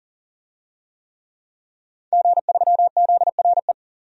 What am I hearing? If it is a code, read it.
Morse code, heard as a keyed tone, about 40 wpm, 700 Hz. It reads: G3ZRE
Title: DXCC-100 Call Signs - 5 Characters - 40wpm